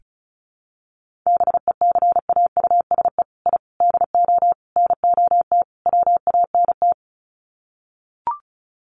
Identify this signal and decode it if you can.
Morse code, heard as a keyed tone, about 34 wpm, 700 Hz. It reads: BECAUSE I DO NOT WANT
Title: Sets of 5 Words Encore 35wpm